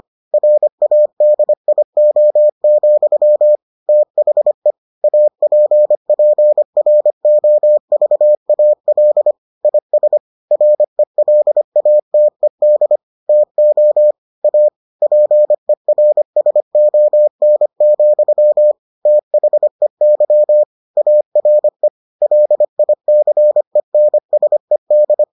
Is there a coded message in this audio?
Yes: Morse code from 0.3 to 25.2 s, heard as a keyed tone, about 25 wpm, 600 Hz. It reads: RADIO, THE APPROVAL IS RELATED TO A PERSON, THEY ARE LICENSED